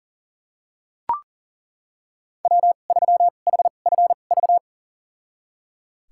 Morse code, heard as a keyed tone, about 38 wpm, spaced as for 28 wpm, 700 Hz. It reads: W3HFV
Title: USA Call Signs - 5 Characters - 2x Character Spacing - 40wpm